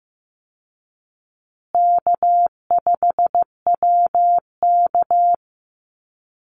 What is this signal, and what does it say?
Morse code, heard as a keyed tone, about 15 wpm, 700 Hz. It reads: K5WK
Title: Easy Call Signs Repeated 3x - 15wpm